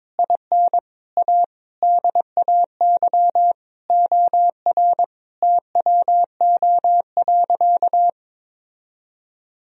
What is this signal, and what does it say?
Morse code, heard as a keyed tone, about 22 wpm, 700 Hz. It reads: IN A DAY OR TWO.
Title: Sentences from Top 1000 Words - Review Part 1 - 22wpm